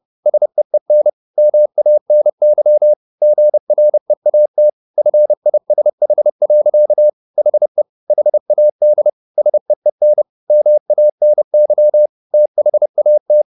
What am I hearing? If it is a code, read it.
Morse code, heard as a keyed tone, about 30 wpm, 600 Hz. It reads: SEEN MANY GREAT FISH. HE HAD SEEN MANY THAT
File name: the old man and the sea / the old man and the sea 1 30wpm